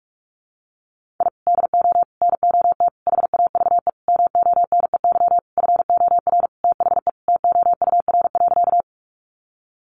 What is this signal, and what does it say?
Morse code, heard as a keyed tone, about 44 wpm, 700 Hz. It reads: I DO NOT HAVE MONEY FOR THE TOUR.
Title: Sentences from Top 1400 Words 45wpm